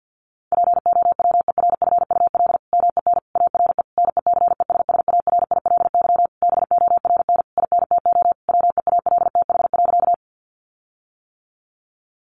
Morse code, heard as a keyed tone, about 50 wpm, 700 Hz. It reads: POWERFUL MEN ARE NECESSARILY BORN INTO WEALTH.
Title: Sentences from Top 1400 Words 50wpm